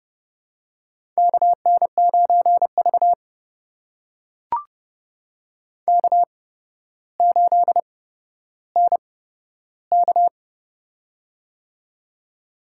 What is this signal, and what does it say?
Morse code, heard as a keyed tone, about 30 wpm, spaced as for 13 wpm, 700 Hz. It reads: KN9V K8NK
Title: USA Call Signs - 8x Character Spacing - 30wpm